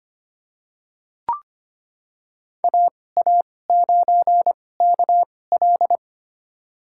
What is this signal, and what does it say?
Morse code, heard as a keyed tone, about 25 wpm, spaced as for 18 wpm, 700 Hz. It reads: AA9KL
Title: USA Call Signs - 5 Characters - 2x Character Spacing - 25wpm